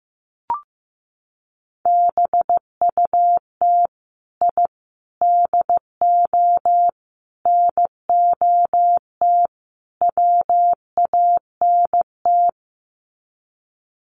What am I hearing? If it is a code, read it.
Morse code, heard as a keyed tone, about 15 wpm, 700 Hz. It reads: BUT I DO NOT WANT
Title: Sets of 5 Words 15wpm